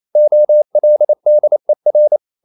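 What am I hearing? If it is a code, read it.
Morse code, heard as a keyed tone, about 28 wpm, 600 Hz. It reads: OLDER
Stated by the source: Morse code practice at 28 wpm, text of Ken Follett's novel Never in English, from CW med Gnister.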